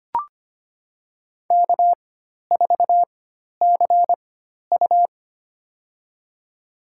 Morse code, heard as a keyed tone, about 25 wpm, spaced as for 12 wpm, 700 Hz. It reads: K4CU
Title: USA Call Signs - 4x Character Spacing - 25wpm